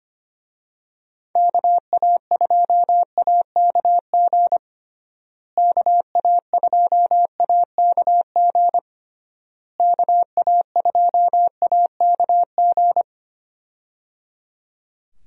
Morse code, heard as a keyed tone, about 25 wpm, 700 Hz. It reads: KA2AKG KA2AKG KA2AKG
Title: USA Call Signs - 6 Characters - Repeated 3x - 25wpm